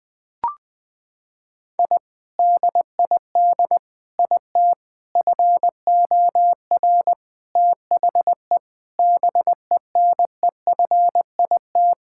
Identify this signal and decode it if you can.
Morse code, heard as a keyed tone, about 20 wpm, 700 Hz. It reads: I DID IT FOR THE BENEFIT
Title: Sentences from Top 1000 Words - Review Part 6 - 20wpm